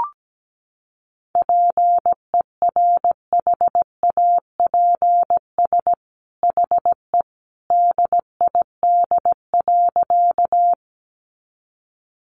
Morse code, heard as a keyed tone, about 17 wpm, 700 Hz. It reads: PERHAPS HE DID.
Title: Sentences from Top 500 Words 17wpm